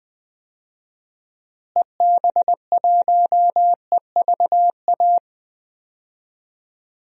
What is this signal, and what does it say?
Morse code, heard as a keyed tone, about 20 wpm, 700 Hz. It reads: EB1EVA
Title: DXCC-100 Call Signs - 6 Characters - Repeated 2x - 20wpm